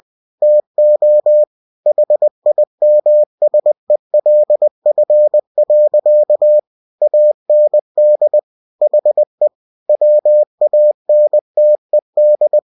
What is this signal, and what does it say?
Morse code, heard as a keyed tone, about 20 wpm, 600 Hz. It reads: TO HIMSELF. AND HE WANTED
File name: cw-books / winnie-the-pooh / winnie-20wpm-page-0042